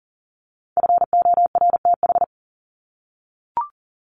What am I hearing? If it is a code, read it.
Morse code, heard as a keyed tone, about 40 wpm, 700 Hz. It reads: FORTH
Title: Top 1751-2000 Words 40wpm